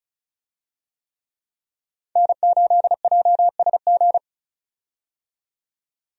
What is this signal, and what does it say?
Morse code, heard as a keyed tone, about 36 wpm, 700 Hz. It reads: N8JSG
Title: USA Call Signs - 5 Characters - 35wpm